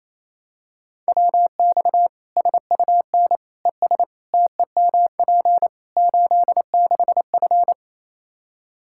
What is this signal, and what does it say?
Morse code, heard as a keyed tone, about 28 wpm, 700 Hz. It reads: WX SUN ES TEMP 86F
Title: K1USN SST QSO Practice 28wpm